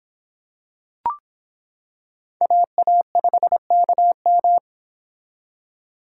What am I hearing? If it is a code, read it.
Morse code, heard as a keyed tone, about 25 wpm, 700 Hz. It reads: AA5KM